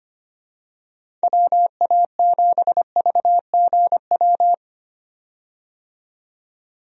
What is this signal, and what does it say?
Morse code, heard as a keyed tone, about 25 wpm, 700 Hz. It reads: WA7VGW
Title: USA Call Signs - 6 Characters - 25wpm